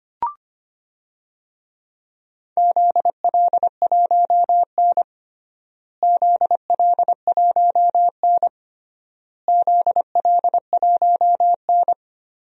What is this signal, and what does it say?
Morse code, heard as a keyed tone, about 25 wpm, 700 Hz. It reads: ZL1N ZL1N ZL1N
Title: DXCC-100 Call Signs - 4 Characters - Repeated 3x - 25wpm